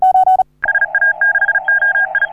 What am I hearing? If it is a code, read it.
Morse code, heard as a keyed tone, about 36 wpm, 1600 Hz. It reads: SA6OI
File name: CW / HA3OU RHOBCW REC